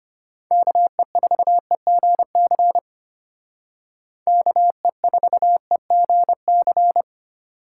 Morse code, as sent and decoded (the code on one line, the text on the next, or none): -.- . ....- . --. -.-. / -.- . ....- . --. -.-.
KE4EGC KE4EGC